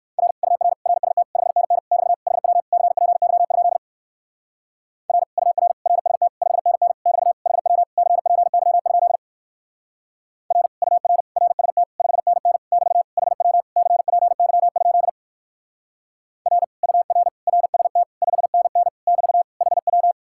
Morse code, heard as a keyed tone, about 62 wpm, 700 Hz. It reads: R UR RST 5NN <BT> HW CPY? R UR RST 5NN <BT> HW CPY? R UR RST 5NN <BT> HW CPY? R UR RST 5NN <BT> HW